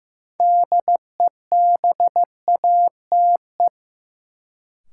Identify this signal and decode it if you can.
Morse code, heard as a keyed tone, about 15 wpm, 700 Hz. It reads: DEBATE
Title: Sentences from Top 1100 Words - New Word Drill Alt - 15wpm